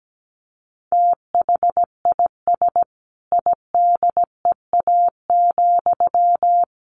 Morse code, heard as a keyed tone, about 17 wpm, 700 Hz. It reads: THIS IDEA,
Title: Sentences from Top 1300 Words 17wpm